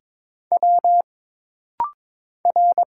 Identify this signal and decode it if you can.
Morse code, heard as a keyed tone, about 22 wpm, 700 Hz. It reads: W R
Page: This rapid-fire timing is geared to push your ICR (Instant Character Recognition) skills.